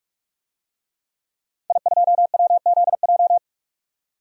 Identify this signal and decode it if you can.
Morse code, heard as a keyed tone, about 42 wpm, 700 Hz. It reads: I2WZJ
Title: DXCC-100 Call Signs - 5 Characters - 45wpm